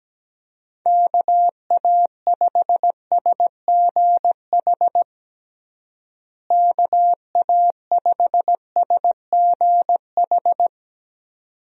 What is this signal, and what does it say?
Morse code, heard as a keyed tone, about 17 wpm, 700 Hz. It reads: KA5SGH KA5SGH
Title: USA Call Signs - 6 Characters - Repeated 2x - 17wpm